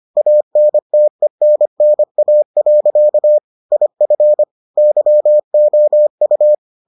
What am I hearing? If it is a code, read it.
Morse code, heard as a keyed tone, about 25 wpm, 600 Hz. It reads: ANTENNA. IF YOU